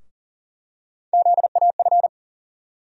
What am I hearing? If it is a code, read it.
Morse code, heard as a keyed tone, about 40 wpm, 700 Hz. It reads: ZAF
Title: USA Call Sign Postfixes - 3 Letters 40wpm